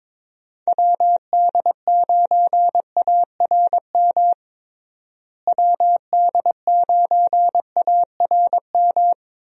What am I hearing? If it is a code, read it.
Morse code, heard as a keyed tone, about 22 wpm, 700 Hz. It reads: WD9ARM WD9ARM
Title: USA Call Signs - 6 Characters - Repeated 2x - 22wpm